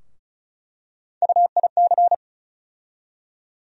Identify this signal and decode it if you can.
Morse code, heard as a keyed tone, about 34 wpm, 700 Hz. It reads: UIC
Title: USA Call Sign Postfixes - 3 Letters 35wpm